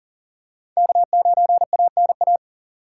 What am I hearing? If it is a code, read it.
Morse code, heard as a keyed tone, about 40 wpm, 700 Hz. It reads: K9ANA
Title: USA Call Signs - 5 Characters - 2x Character Spacing - 40wpm